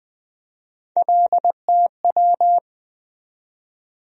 Morse code, heard as a keyed tone, about 20 wpm, 700 Hz. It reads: LTW
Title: USA Call Sign Postfixes - 3 Letters 20wpm